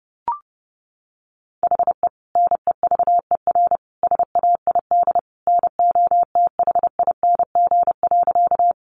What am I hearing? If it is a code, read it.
Morse code, heard as a keyed tone, about 30 wpm, 700 Hz. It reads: HE NEVER SAID NOTHING.
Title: Sentences from Top 500 Words 30wpm